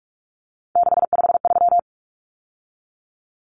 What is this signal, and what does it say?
Morse code, heard as a keyed tone, about 44 wpm, 700 Hz. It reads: 653
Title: Numbers - 3-Digit 45wpm